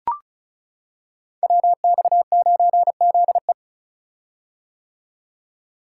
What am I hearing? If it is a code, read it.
Morse code, heard as a keyed tone, about 35 wpm, 700 Hz. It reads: WX9ZE
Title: Code Groups - 5 Letter:Numbers - 35wpm